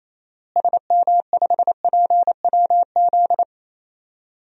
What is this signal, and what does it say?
Morse code, heard as a keyed tone, about 28 wpm, 700 Hz. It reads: SM5PWZ
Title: DXCC-100 Call Signs - 6 Characters - Repeated 3x - 28wpm